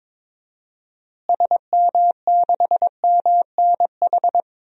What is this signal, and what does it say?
Morse code, heard as a keyed tone, about 22 wpm, 700 Hz. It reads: SM6MNH
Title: DXCC-100 Call Signs - 6 Characters - Repeated 2x - 22wpm